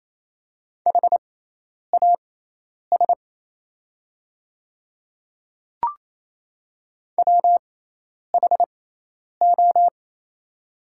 Morse code, heard as a keyed tone, about 27 wpm, spaced as for 10 wpm, 690 Hz. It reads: HAS WHO
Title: 6x Character Spacing - Top 100 Words - 28wpm